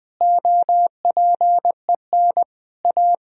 Morse code, heard as a keyed tone, about 20 wpm, 700 Hz. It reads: OPEN A
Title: Sentences from Top 1000 Words - Review Part 3 - 20wpm